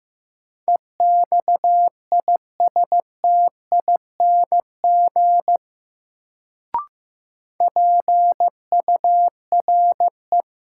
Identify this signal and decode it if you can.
Morse code, heard as a keyed tone, about 15 wpm, 700 Hz. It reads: EXISTING PURE
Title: Sentences from Top 1500 Words - New Word Drill - 15wpm